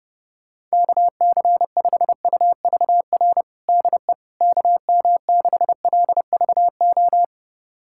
Morse code, heard as a keyed tone, about 30 wpm, 700 Hz. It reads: KC5UVR DE KM6LVO
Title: USA Callsign DE Callsign 30wpm